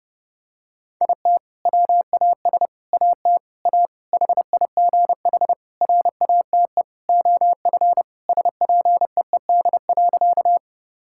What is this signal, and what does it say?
Morse code, heard as a keyed tone, about 30 wpm, 700 Hz. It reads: IT WAS AT A HIGH RATE OF SPEED.
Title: Sentences from Top 1000 Words - Review Part 5 - 30wpm